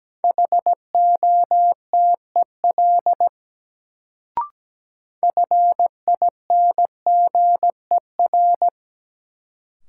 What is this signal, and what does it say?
Morse code, heard as a keyed tone, about 17 wpm, 700 Hz. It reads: HOTEL FINGER
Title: Sentences from Top 1600 Words - New Word Drill Alt - 17wpm